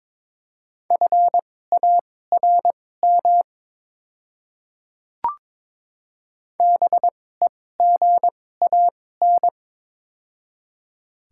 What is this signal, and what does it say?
Morse code, heard as a keyed tone, about 22 wpm, spaced as for 16 wpm, 700 Hz. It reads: FARM BEGAN
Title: 2x Character Spacing - Top 500 Words - 22wpm